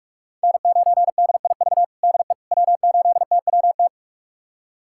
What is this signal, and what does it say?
Morse code, heard as a keyed tone, about 46 wpm, 700 Hz. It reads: N9DIV DE W8TWT
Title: USA Callsign DE Callsign - 5 Characters - 45wpm